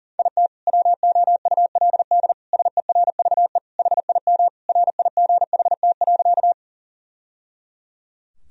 Morse code, heard as a keyed tone, about 40 wpm, 700 Hz. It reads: IT WOULD SERVE HIM RIGHT.
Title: Sentences from Top 1000 Words - Review Part 2 - 40wpm